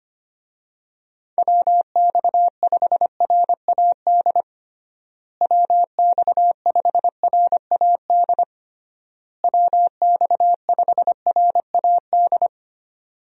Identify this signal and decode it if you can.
Morse code, heard as a keyed tone, about 25 wpm, 700 Hz. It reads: WX5RAD WX5RAD WX5RAD
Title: USA Call Signs - 6 Characters - Repeated 3x - 25wpm